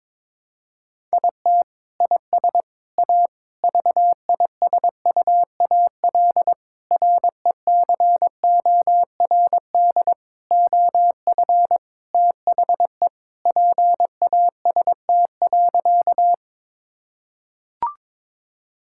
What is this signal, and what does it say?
Morse code, heard as a keyed tone, about 22 wpm, 700 Hz. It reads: IT IS A VISUAL RECORD OF THE PAST.
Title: Sentences from Top 1000 Words - Review Part 6 - 22wpm